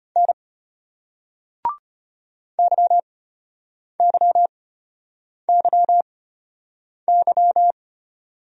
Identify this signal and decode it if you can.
Morse code, heard as a keyed tone, about 34 wpm, spaced as for 8 wpm, 700 Hz. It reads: N YYYY